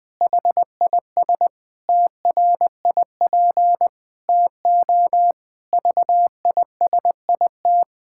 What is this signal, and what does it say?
Morse code, heard as a keyed tone, about 21 wpm, 700 Hz. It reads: HIS TRIP TO VISIT
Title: Pangram Challenge v2 - 20wpm